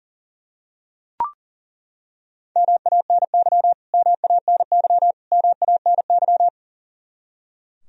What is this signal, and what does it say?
Morse code, heard as a keyed tone, about 37 wpm, 700 Hz. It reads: MANY MANY MANY
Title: Top 100 Words Repeated 40wpm